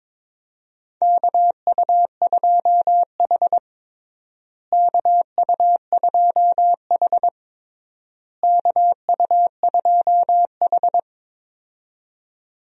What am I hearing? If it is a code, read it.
Morse code, heard as a keyed tone, about 22 wpm, 700 Hz. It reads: KU2H KU2H KU2H
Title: USA Call Signs - 4 Characters - Repeated 3x - 22wpm